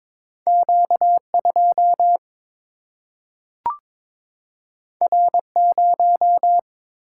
Morse code, heard as a keyed tone, about 22 wpm, 700 Hz. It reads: Q 2 R 0
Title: Single-Letter Single-Number 22wpm